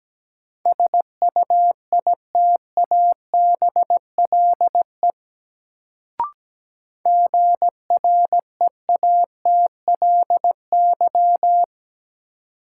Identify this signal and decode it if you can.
Morse code, heard as a keyed tone, about 17 wpm, 700 Hz. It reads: SUITABLE GREATLY
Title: Sentences from Top 1400 Words - New Word Drill Alt - 17wpm